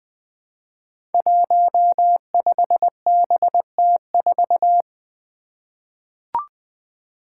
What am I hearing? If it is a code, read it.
Morse code, heard as a keyed tone, about 20 wpm, 700 Hz. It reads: 15BT4
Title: Five Challenging Letter-Number Combinations 20wpm